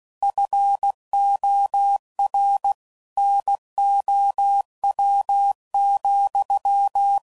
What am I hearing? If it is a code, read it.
Morse code, heard as a keyed tone, about 16 wpm, 800 Hz. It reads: FOR NOW,
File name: CW Code Groups @15WPM / Technical Subjects / Gates Intro